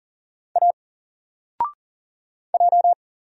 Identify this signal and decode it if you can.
Morse code, heard as a keyed tone, about 40 wpm, 700 Hz. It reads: A J
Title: Single Letter-Number 40wpm